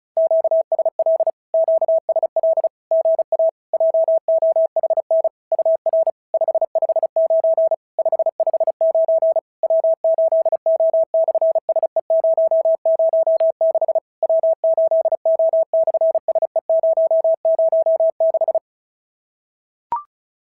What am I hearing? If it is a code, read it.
Morse code, heard as a keyed tone, about 33 wpm, 650 Hz. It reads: QSL QSL GA JOHN UR 559 559 W8O/SE006 W8O/SE006
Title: SOTA QSO Practice 35wpm